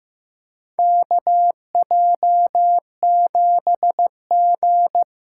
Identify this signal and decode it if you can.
Morse code, heard as a keyed tone, about 15 wpm, 700 Hz. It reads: KJ7G
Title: USA Call Signs - 6 Characters - 8x Character Spacing - 15wpm